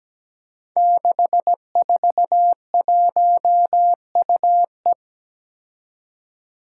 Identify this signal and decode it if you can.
Morse code, heard as a keyed tone, about 17 wpm, 700 Hz. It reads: 641UE
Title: Five Challenging Letter-Number Combinations 17wpm